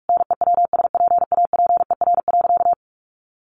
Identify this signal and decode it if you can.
Morse code, heard as a keyed tone, about 44 wpm, 700 Hz. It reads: NEWSPAPER.
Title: Sentences from Top 1400 Words 45wpm